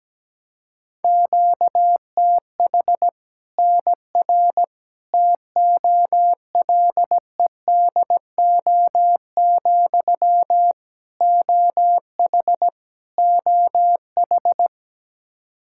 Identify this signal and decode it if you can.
Morse code, heard as a keyed tone, about 17 wpm, 700 Hz. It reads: QTH NR TOLEDO, OH OH
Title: Practice QSO - Exchange 1 - Beginning - 17wpm